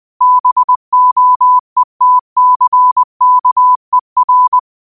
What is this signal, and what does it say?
Morse code, heard as a keyed tone, about 20 wpm, 1000 Hz. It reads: BOETCKER
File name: qotdfiles / 20wpm